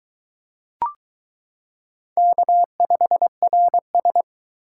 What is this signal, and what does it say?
Morse code, heard as a keyed tone, about 22 wpm, 700 Hz. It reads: K5RS